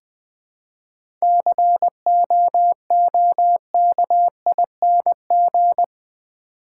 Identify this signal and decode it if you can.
Morse code, heard as a keyed tone, about 20 wpm, 700 Hz. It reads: COOKING
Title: 7 Letter Words - First 3 Letters Given - 20wpm